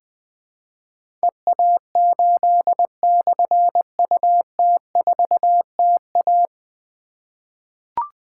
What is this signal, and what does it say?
Morse code, heard as a keyed tone, about 20 wpm, 700 Hz. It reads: EA8/UT4TA
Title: Difficult Call Signs 20wpm